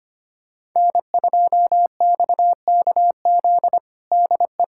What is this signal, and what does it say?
Morse code, heard as a keyed tone, about 25 wpm, 700 Hz. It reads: N2XKZ DE
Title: USA Callsign DE Callsign - 5 Characters - 25wpm